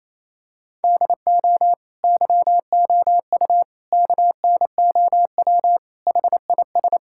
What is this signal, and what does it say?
Morse code, heard as a keyed tone, about 28 wpm, 700 Hz. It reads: DO YOU KNOW HIS
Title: Sentences from Top 1600 Words 28wpm